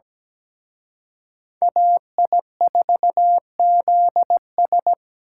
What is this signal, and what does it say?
Morse code, heard as a keyed tone, about 20 wpm, 700 Hz. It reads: AI4ZS
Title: Speed Racing - USA 2x2 Call Signs 17wpm